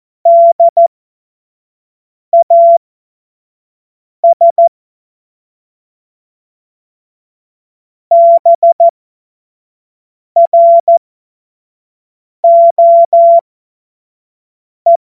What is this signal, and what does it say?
Morse code, heard as a keyed tone, about 14 wpm, spaced as for 5 wpm, 680 Hz. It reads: DAS BROE
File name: Morsekurs / Witze / Witz 21. Dezember 2021, 50 F70